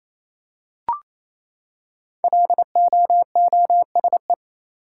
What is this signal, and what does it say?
Morse code, heard as a keyed tone, about 28 wpm, 700 Hz. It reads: LOOSE
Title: Five Letter Words - 28wpm